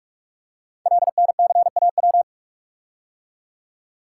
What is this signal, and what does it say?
Morse code, heard as a keyed tone, about 44 wpm, 700 Hz. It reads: LNCAW
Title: Code Groups - 5 Letters - 45wpm